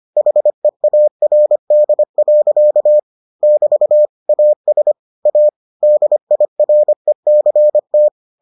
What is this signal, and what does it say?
Morse code, heard as a keyed tone, about 25 wpm, 600 Hz. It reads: HEARD. <BT> AS A DIRECT